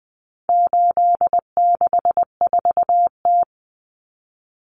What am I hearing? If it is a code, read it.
Morse code, heard as a keyed tone, about 20 wpm, 700 Hz. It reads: 864T
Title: Common Cut Numbers - 4-Digit 20wpm